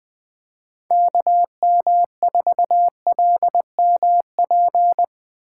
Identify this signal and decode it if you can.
Morse code, heard as a keyed tone, about 20 wpm, 700 Hz. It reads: KM4LMP
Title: USA Call Signs - 8x Character Spacing - 20wpm